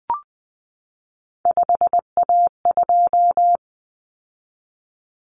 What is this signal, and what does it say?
Morse code, heard as a keyed tone, about 20 wpm, 700 Hz. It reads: 5A2
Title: Common Cut Numbers - 3-Digit 20wpm